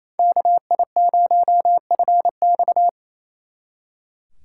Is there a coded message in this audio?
Yes: Morse code from 0.2 to 2.9 s, heard as a keyed tone, about 36 wpm, 700 Hz. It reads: KI0FX